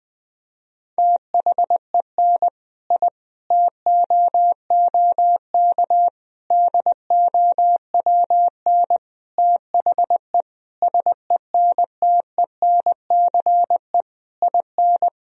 Morse code, heard as a keyed tone, about 21 wpm, 700 Hz. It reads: THEN I TOOK DOWN THE SENTENCE IN
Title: Sentences from Top 1000 Words - Review Part 1 - 20wpm